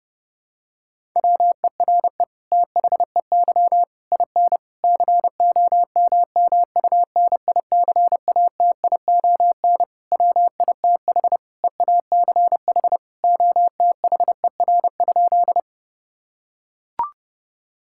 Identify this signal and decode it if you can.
Morse code, heard as a keyed tone, about 30 wpm, 700 Hz. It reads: WERE THEY IN COMMUNICATION WITH EACH OTHER?
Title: Sentences from Top 1000 Words - Review Part 6 - 30wpm